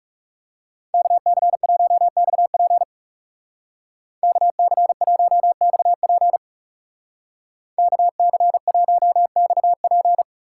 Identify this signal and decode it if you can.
Morse code, heard as a keyed tone, about 37 wpm, 700 Hz. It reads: KC1XP KC1XP KC1XP
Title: Speed Racing - USA Call Signs 30wpm